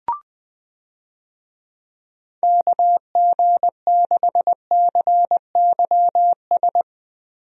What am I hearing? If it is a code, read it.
Morse code, heard as a keyed tone, about 20 wpm, 700 Hz. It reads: KG6CYS
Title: USA Call Signs - 6 Characters - Repeated 2x - 20wpm